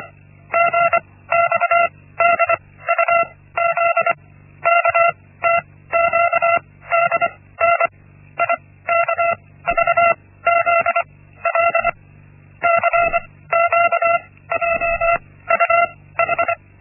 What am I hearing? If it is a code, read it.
Morse code, heard as a keyed tone, about 24 wpm, spaced as for 16 wpm, 680 Hz. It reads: GXDUZ KTQDN IKVZL CQJUH
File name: Linksmakalnis-1